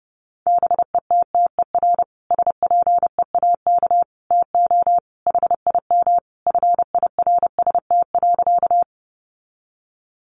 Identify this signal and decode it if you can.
Morse code, heard as a keyed tone, about 30 wpm, 700 Hz. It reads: BETTER SPEAK TO HIM FIRST.
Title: Sentences from Top 1100 Words 30wpm